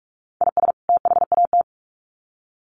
Morse code, heard as a keyed tone, about 44 wpm, 700 Hz. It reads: IS THAT
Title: Sets of 2 Words 45wpm